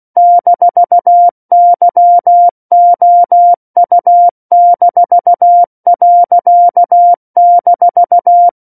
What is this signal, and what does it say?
Morse code, heard as a keyed tone, about 16 wpm, 700 Hz. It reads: -YOU-.-
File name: 03-Chapter-03 16 WPM